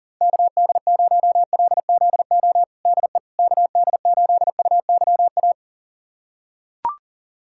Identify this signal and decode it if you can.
Morse code, heard as a keyed tone, about 40 wpm, 700 Hz. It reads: KD0LZO DE KD8UYA
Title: USA Callsign DE Callsign - 6 Characters - 40wpm